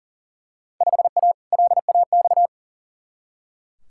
Morse code, heard as a keyed tone, about 40 wpm, 700 Hz. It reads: 5A LAX
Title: Field Day Exchanges - 40wpm